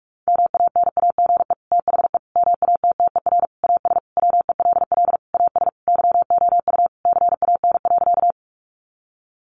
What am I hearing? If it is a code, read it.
Morse code, heard as a keyed tone, about 44 wpm, 700 Hz. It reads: MANAGE THE MATTER AS WELL AS YOU CAN.
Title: Sentences from Top 1400 Words 45wpm